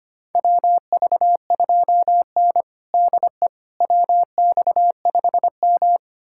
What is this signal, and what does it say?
Morse code, heard as a keyed tone, about 25 wpm, 700 Hz. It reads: WV2N DE WX5M
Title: USA Callsign DE Callsign - 4 Characters - 25wpm